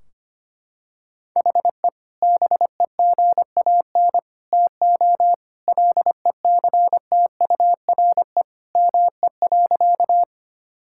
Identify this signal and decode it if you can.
Morse code, heard as a keyed tone, about 25 wpm, 700 Hz. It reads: HE BEGAN TO LECTURE ME.
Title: Sentences from Top 1600 Words 25wpm